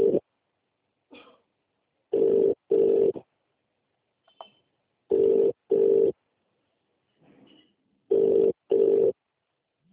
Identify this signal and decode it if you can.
Ringback tone (UK), 400 + 450 Hz.